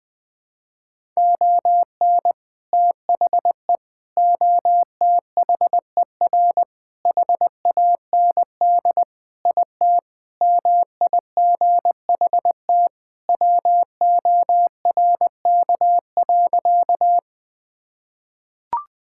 Morse code, heard as a keyed tone, about 20 wpm, 700 Hz. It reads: ON THE OTHER HAND IT MIGHT WORK.
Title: Sentences with 3-Word Phrases - 20wpm